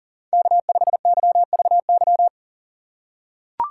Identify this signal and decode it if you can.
Morse code, heard as a keyed tone, about 37 wpm, 700 Hz. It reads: K5YVY
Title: K1USN SST QSO Practice - 40wpm